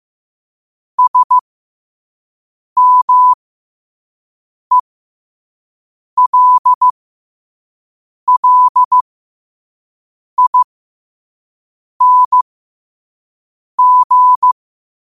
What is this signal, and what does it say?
Morse code, heard as a keyed tone, about 15 wpm, spaced as for 5 wpm, 1000 Hz. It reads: SMELLING
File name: qotdfiles / 5wpm